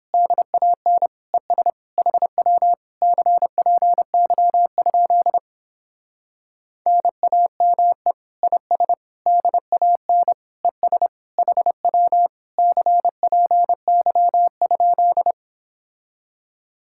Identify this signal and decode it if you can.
Morse code, heard as a keyed tone, about 29 wpm, 700 Hz. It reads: DAN ES HW CPY? NAME IS DAN ES HW CPY?